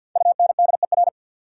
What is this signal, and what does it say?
Morse code, heard as a keyed tone, about 53 wpm, 690 Hz. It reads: UNDER
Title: Prefixes and Suffixes 50wpm